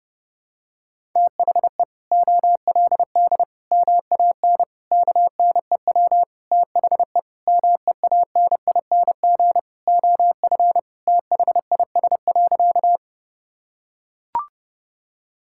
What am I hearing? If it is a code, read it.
Morse code, heard as a keyed tone, about 32 wpm, 700 Hz. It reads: THE OLD MAN KNEW THE MEANING OF THIS.
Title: Sentences from Top 1000 Words - Review Part 6 - 30wpm